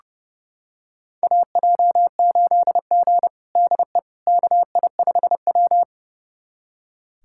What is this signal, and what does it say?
Morse code, heard as a keyed tone, about 30 wpm, 700 Hz. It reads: AJ8G DE KI5W
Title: USA Callsign DE Callsign - 4 Characters - 30wpm